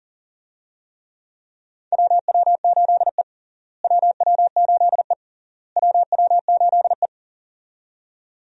Morse code, heard as a keyed tone, about 40 wpm, 700 Hz. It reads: WW8E WW8E WW8E
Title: USA Call Signs - 4 Characters - Repeated 3x - 40wpm